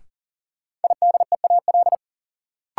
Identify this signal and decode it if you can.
Morse code, heard as a keyed tone, about 37 wpm, 700 Hz. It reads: IDEAL
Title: Sentences from Top 1400 Words - New Word Drill Alt - 40wpm